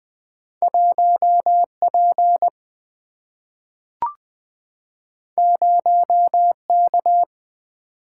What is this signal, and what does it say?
Morse code, heard as a keyed tone, about 20 wpm, 700 Hz. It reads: 1 P 0 K
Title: Single-Letter Single-Number 20wpm